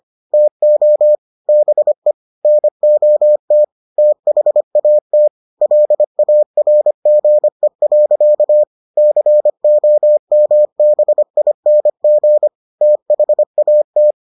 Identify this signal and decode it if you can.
Morse code, heard as a keyed tone, about 25 wpm, 600 Hz. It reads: TO BE NOT THAT LARGE. COMBING THAT